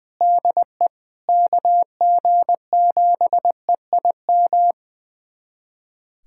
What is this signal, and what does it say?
Morse code, heard as a keyed tone, about 20 wpm, 700 Hz. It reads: DE KG7EIM
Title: USA Callsign DE Callsign 20wpm